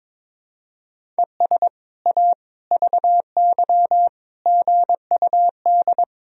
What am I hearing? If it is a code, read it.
Morse code, heard as a keyed tone, about 22 wpm, 700 Hz. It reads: ES A VY GUD
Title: Sentences from Top 200 Words and Abbreviations 22wpm